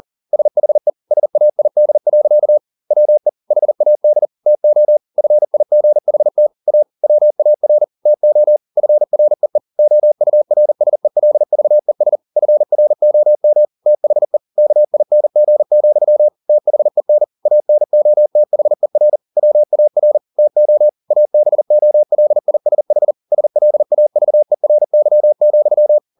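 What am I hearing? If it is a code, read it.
Morse code, heard as a keyed tone, about 40 wpm, 600 Hz. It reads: SHE SAID. WE HAD TO FIGHT A WAR TO FREE OURSELVES FROM THE KING, THEN ANOTHER WAR TO ABOLISH SLAVERY,
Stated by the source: Morse code at 40 wpm, practice text, CW med Gnister.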